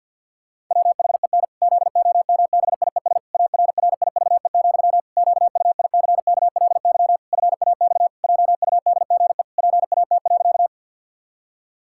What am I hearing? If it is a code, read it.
Morse code, heard as a keyed tone, about 51 wpm, 700 Hz. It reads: WHEN ZOMBIES ARRIVE, QUICKLY FAX JUDGE PAT.
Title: Pangram Challenge v2 - 50wpm